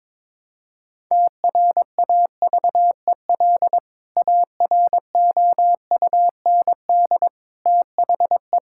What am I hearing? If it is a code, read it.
Morse code, heard as a keyed tone, about 22 wpm, 700 Hz. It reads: TRAVEL AROUND THE
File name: practice / 22wpm / Sentences with 3-Word Phrases - 22wpm